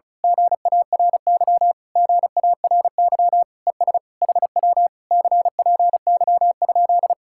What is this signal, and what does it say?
Morse code, heard as a keyed tone, about 33 wpm, 700 Hz. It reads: GARY GARY ES HW CPY?
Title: Speed-Racing Common QSO Phrases Encore - 35wpm